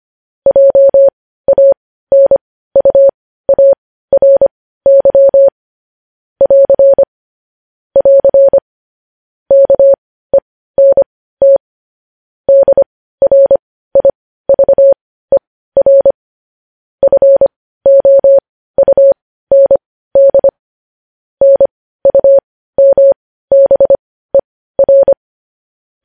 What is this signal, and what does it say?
Morse code, heard as a keyed tone, about 25 wpm, spaced as for 15 wpm, 550 Hz. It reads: JANUARY <AR> <AR> KENT DRIVER FOUND NUMBER
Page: news-headlines-in-morse-code-at-15-wpm